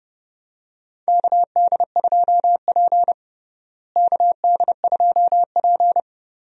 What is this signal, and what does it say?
Morse code, heard as a keyed tone, about 32 wpm, 700 Hz. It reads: KD2P KD2P
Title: USA Call Signs - 4 Characters - Repeated 3x - 30wpm